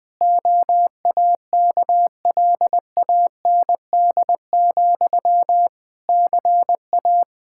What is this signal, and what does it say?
Morse code, heard as a keyed tone, about 20 wpm, 700 Hz. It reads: OAKLAND, CA
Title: K1USN SST QSO Practice 20wpm